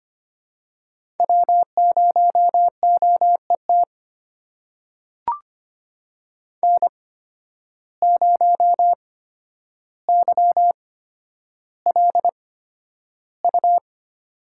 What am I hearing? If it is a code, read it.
Morse code, heard as a keyed tone, about 25 wpm, spaced as for 11 wpm, 700 Hz. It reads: W0OET N0YLU
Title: USA Call Signs - 5 Characters - 8x Character Spacing - 25wpm